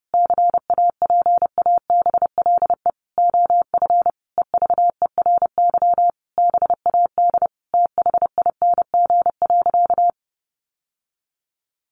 Morse code, heard as a keyed tone, about 30 wpm, 700 Hz. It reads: CAPABLE OF EVERY BAD THING.
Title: Sentences from Top 1200 Words 30wpm